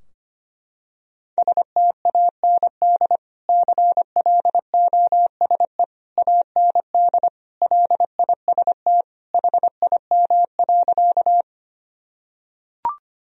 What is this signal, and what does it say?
Morse code, heard as a keyed tone, about 25 wpm, 700 Hz. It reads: STAND CLOSE AND LIST HIM.
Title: Sentences from Top 1000 Words - Review Part 2 - 25wpm